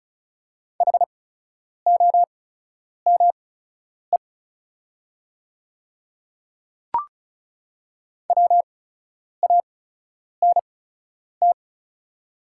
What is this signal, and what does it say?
Morse code, heard as a keyed tone, about 36 wpm, spaced as for 10 wpm, 700 Hz. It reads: HOME WANT
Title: 8x Character Spacing - Top 300 Words - 35wpm